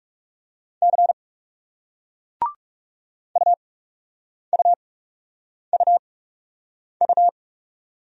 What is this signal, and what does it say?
Morse code, heard as a keyed tone, about 39 wpm, spaced as for 8 wpm, 700 Hz. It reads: C UUUU